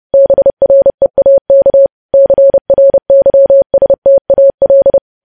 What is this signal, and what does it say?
Morse code, heard as a keyed tone, about 30 wpm, 550 Hz. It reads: BREAK CRYSTAL
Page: news headlines in morse code at 30 wpm